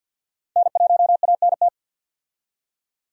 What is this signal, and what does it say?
Morse code, heard as a keyed tone, about 50 wpm, 700 Hz. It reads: N1ANT
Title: USA Call Signs - 5 Characters - 50wpm